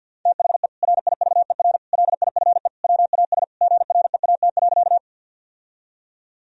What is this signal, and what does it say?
Morse code, heard as a keyed tone, about 52 wpm, 700 Hz. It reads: THE RIVER LIFE WAS GREAT.
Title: Sentences from Top 1000 Words - Review Part 1 - 50wpm